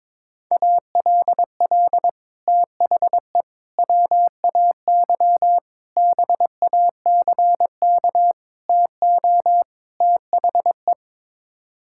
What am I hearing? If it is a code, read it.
Morse code, heard as a keyed tone, about 22 wpm, 700 Hz. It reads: ALL THE WAY BACK TO THE
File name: practice / 22wpm / Sets of 6 Words - 22wpm